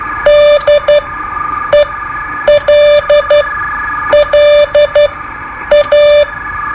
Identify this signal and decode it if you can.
Morse code, heard as a keyed tone, about 12 wpm, spaced as for 8 wpm, 590 Hz. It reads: DELLA
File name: qso-01